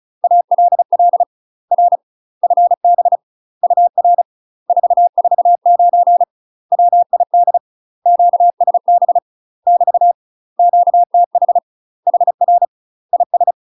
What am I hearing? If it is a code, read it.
Morse code, heard as a keyed tone, about 36 wpm, 700 Hz. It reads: ALL R FB UR 449 WID QSB <BT> QTH HR IS
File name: UK MASTER QSOs TRAINING / 35WPM / UK MASTER QSOs 0029